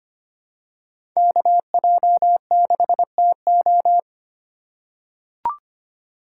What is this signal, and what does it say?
Morse code, heard as a keyed tone, about 25 wpm, 700 Hz. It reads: KJ6TO
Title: USA Call Signs - 5 Characters - 4x Character Spacing - 25wpm